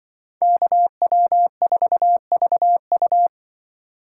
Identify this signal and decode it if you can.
Morse code, heard as a keyed tone, about 25 wpm, 700 Hz. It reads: KW4VU
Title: Speed Racing - USA 2x2 Call Signs 20wpm